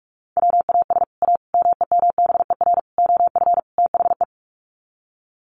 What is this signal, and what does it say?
Morse code, heard as a keyed tone, about 44 wpm, 700 Hz. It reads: WAS A MEMBER OF THE
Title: Sets of 5 Words 45wpm